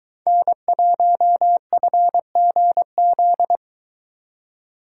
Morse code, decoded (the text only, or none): N1FGZ